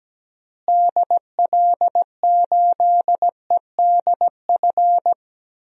Morse code, heard as a keyed tone, about 17 wpm, 700 Hz. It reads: DL8EDF
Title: DXCC-100 Call Signs - 6 Characters - Repeated 3x - 17wpm